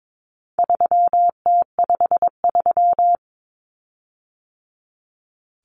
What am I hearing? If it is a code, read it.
Morse code, heard as a keyed tone, about 22 wpm, 700 Hz. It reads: 3T53
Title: Common Cut Numbers - 4-Digit 22wpm